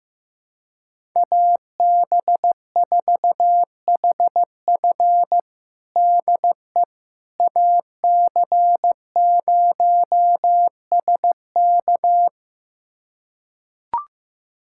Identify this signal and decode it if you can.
Morse code, heard as a keyed tone, about 15 wpm, 700 Hz. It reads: AB4HF DE AC0SK
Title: USA Callsign DE Callsign - 5 Characters - 15wpm